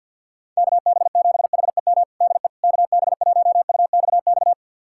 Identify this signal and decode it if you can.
Morse code, heard as a keyed tone, about 53 wpm, 700 Hz. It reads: KB7HEM DE KB1UXX
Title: USA Callsign DE Callsign - 6 Characters - 50wpm